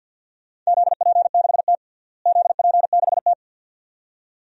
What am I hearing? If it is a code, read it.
Morse code, heard as a keyed tone, about 52 wpm, 700 Hz. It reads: ZP6T ZP6T
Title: DXCC-100 Call Signs - 4 Characters - Repeated 2x - 50wpm